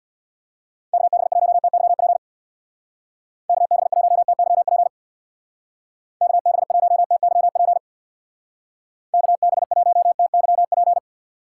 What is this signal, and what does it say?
Morse code, heard as a keyed tone, about 65 wpm, 700 Hz. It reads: KB1TYP KB1TYP KB1TYP KB1TYP